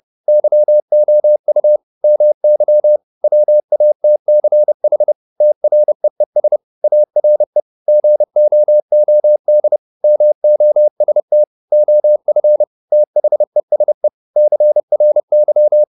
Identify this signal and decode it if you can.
Morse code, heard as a keyed tone, about 30 wpm, 600 Hz. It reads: YOU MY WATCH TREES ARE GOOD MOST OF THESE CRY